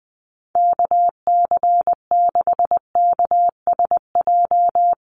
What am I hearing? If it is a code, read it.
Morse code, heard as a keyed tone, about 19 wpm, 700 Hz. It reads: KC6KSJ
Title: USA Call Signs Repeated 3x 20wpm